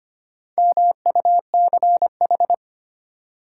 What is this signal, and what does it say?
Morse code, heard as a keyed tone, about 25 wpm, 700 Hz. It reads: MUCH
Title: Sentences from Top 200 Words - New Word Drill - 25wpm